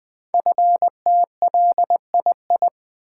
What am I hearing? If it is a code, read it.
Morse code, heard as a keyed tone, about 20 wpm, 700 Hz. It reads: FTLII
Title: Code Groups - 5 Letters - 20wpm